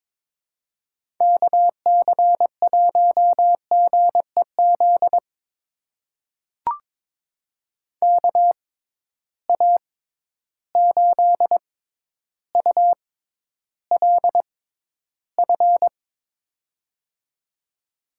Morse code, heard as a keyed tone, about 22 wpm, spaced as for 11 wpm, 700 Hz. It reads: KC1GEZ KA8ULF
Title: USA Call Signs - 6 Characters - 6x Character Spacing - 22wpm